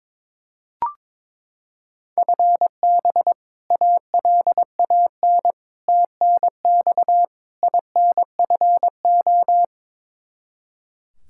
Morse code, heard as a keyed tone, about 22 wpm, 700 Hz. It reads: FB ALAN TNX INFO
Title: Practice QSO - Exchange 3 - More Info - 22wpm